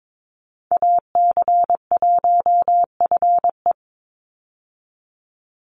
Morse code, heard as a keyed tone, about 22 wpm, 700 Hz. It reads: AC1FE
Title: Call Signs Repeated 3x - 22wpm